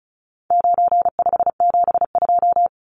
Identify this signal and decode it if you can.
Morse code, heard as a keyed tone, about 34 wpm, 700 Hz. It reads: 9572
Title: Numbers - 4-Digit 35wpm